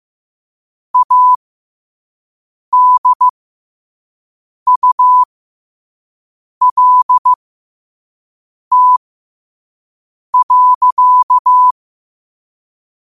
Morse code, heard as a keyed tone, about 15 wpm, spaced as for 5 wpm, 1000 Hz. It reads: ADULT.
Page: New quotes every day in morse code at 5 Words per minute.